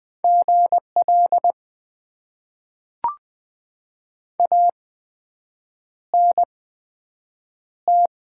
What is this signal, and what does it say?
Morse code, heard as a keyed tone, about 20 wpm, spaced as for 7 wpm, 700 Hz. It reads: GL ANT
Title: Top 90 QSO Elements - 8x Character Spacing - 20wpm